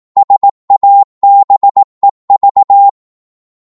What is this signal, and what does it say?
Morse code, heard as a keyed tone, about 18 wpm, 800 Hz. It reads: SABEV